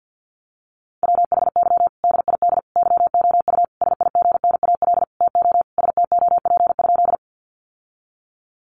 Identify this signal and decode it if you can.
Morse code, heard as a keyed tone, about 50 wpm, 700 Hz. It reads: WHY DID YOU SIGNAL TO STOP?
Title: Sentences from Top 1400 Words 50wpm